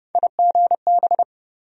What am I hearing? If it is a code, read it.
Morse code, heard as a keyed tone, about 30 wpm, 700 Hz. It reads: IGB
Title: USA Call Sign Postfixes - 3 Letters 30wpm